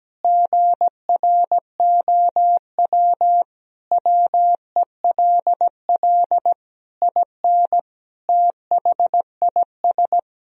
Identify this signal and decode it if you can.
Morse code, heard as a keyed tone, about 17 wpm, 700 Hz. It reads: GROW WELL IN THIS
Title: Sentences from Top 1000 Words - Review Part 2 - 17wpm